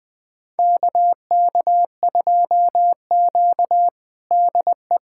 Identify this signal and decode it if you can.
Morse code, heard as a keyed tone, about 20 wpm, 700 Hz. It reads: KK2Q DE
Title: USA Callsign DE Callsign - 4 Characters - 20wpm